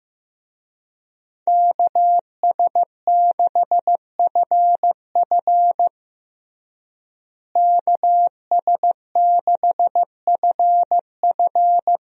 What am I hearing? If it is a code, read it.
Morse code, heard as a keyed tone, about 15 wpm, 700 Hz. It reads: KS6FF KS6FF
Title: USA Call Signs - 5 Characters - Repeated 2x - 15wpm